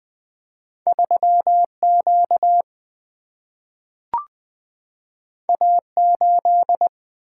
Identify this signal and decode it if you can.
Morse code, heard as a keyed tone, about 21 wpm, 700 Hz. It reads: 3 Q A 8
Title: Single-Letter Single-Number 20wpm